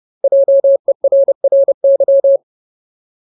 Morse code, heard as a keyed tone, about 30 wpm, 550 Hz. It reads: JERRY
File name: Ultimate Word List / 30WPM Ultimate